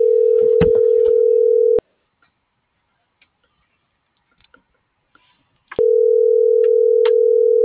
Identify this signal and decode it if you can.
Ringback tone (North America), 440 + 480 Hz.